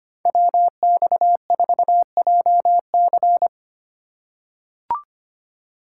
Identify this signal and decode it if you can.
Morse code, heard as a keyed tone, about 25 wpm, 700 Hz. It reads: WX4JC
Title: USA Call Signs - 5 Characters - 3x Character Spacing - 25wpm